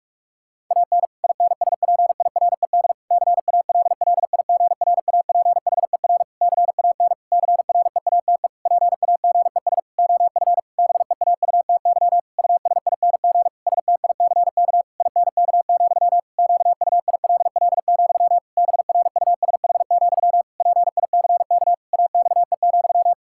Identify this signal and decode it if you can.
Morse code, heard as a keyed tone, about 41 wpm, 700 Hz. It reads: AN INSPIRED CALLIGRAPHER CAN CREATE PAGES OF BEAUTY USING STICK INK, QUILL, BRUSH, PICK AXE,
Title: Pangram Challenge v2 - 45wpm